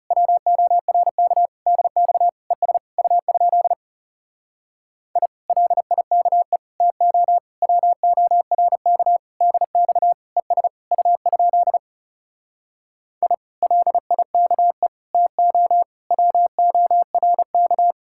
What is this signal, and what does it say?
Morse code, heard as a keyed tone, about 40 wpm, 700 Hz. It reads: WORK DX ES U? I LIKE TO WORK DX ES U? I LIKE TO WORK